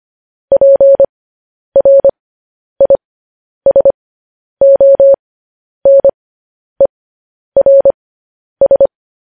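Morse code, heard as a keyed tone, about 25 wpm, spaced as for 10 wpm, 550 Hz. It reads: PRISONERS
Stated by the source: Morse code news headlines sent at 10 wpm for practice.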